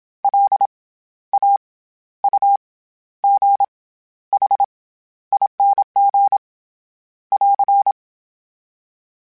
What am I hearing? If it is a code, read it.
Morse code, heard as a keyed tone, about 27 wpm, 800 Hz. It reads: L A U G H ING <AR>
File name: Special INGs-mp3s / Special INGs-mp3s / SlowToFaster ing @25